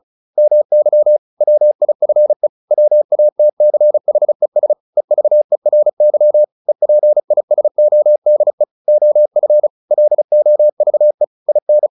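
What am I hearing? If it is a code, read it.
Morse code, heard as a keyed tone, about 34 wpm, 600 Hz. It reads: MY WIFE WATCHES EVERY EPISODE OF LOVE IN
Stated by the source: Morse practice at 35 wpm from CW med Gnister, sending Ken Follett's novel Never in English.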